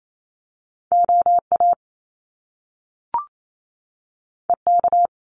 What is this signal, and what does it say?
Morse code, heard as a keyed tone, about 28 wpm, 700 Hz. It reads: OA EK
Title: Top Two Letter Combinations in Words 28wpm